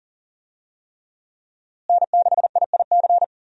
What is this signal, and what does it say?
Morse code, heard as a keyed tone, about 39 wpm, 700 Hz. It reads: N6IIC
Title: USA Call Signs - 5 Characters - Repeated 3x - 40wpm